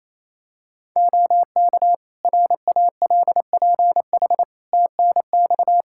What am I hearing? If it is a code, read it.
Morse code, heard as a keyed tone, about 28 wpm, 700 Hz. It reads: OK RALPH TNX
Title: Practice QSO - Exchange 4 - Ending - 28wpm